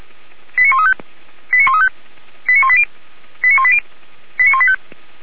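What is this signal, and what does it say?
Five-tone selective call (ZVEI), heard as quick runs of five tones, tones of about 70 ms. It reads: 89135 89135 89180 89180 89185
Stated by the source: ZVEI five-tone selective call 89133, received by SDR, recorded by Feuerwehr Orlamünde (fire brigade).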